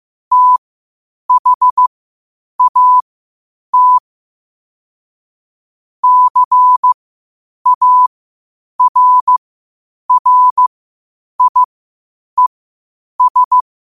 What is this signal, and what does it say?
Morse code, heard as a keyed tone, about 15 wpm, spaced as for 8 wpm, 1000 Hz. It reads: THAT CARRIES
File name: qotdfiles / 8wpm